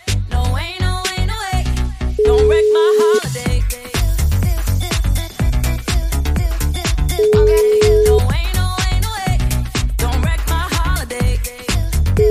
Ringback tone (Europe, 425 Hz), with noise.